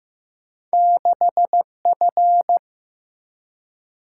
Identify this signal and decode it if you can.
Morse code, heard as a keyed tone, about 15 wpm, 700 Hz. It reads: 6F
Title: Single-Letter Single-Number 15wpm